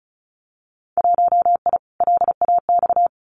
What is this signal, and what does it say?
Morse code, heard as a keyed tone, about 34 wpm, 700 Hz. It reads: 1I LAX
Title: Winter Field Day Exchanges Encore 35wpm